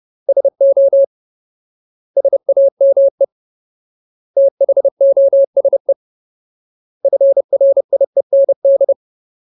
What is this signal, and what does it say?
Morse code, heard as a keyed tone, about 30 wpm, 550 Hz. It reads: SO SAME THOSE FRIEND
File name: Ultimate Word List / 30WPM Ultimate